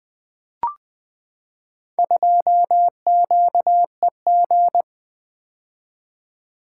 Morse code, heard as a keyed tone, about 20 wpm, 700 Hz. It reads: 2QEG
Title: Four Letter-Number Combinations - 20wpm